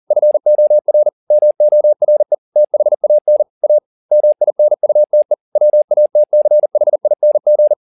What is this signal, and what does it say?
Morse code, heard as a keyed tone, about 40 wpm, 600 Hz. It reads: FOR MORE THAN A MINUTE WATCHING